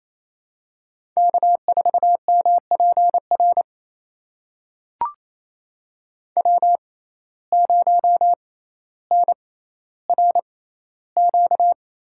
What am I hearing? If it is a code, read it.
Morse code, heard as a keyed tone, about 28 wpm, spaced as for 14 wpm, 700 Hz. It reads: K4MPR W0NRQ
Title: USA Call Signs - 5 Characters - 6x Character Spacing - 28wpm